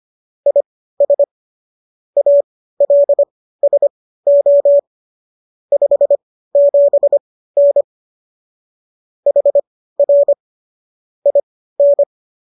Morse code, heard as a keyed tone, about 25 wpm, spaced as for 15 wpm, 570 Hz. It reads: IS ALSO 57N HR IN